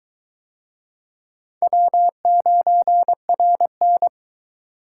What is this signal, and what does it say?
Morse code, heard as a keyed tone, about 22 wpm, 700 Hz. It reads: W9RN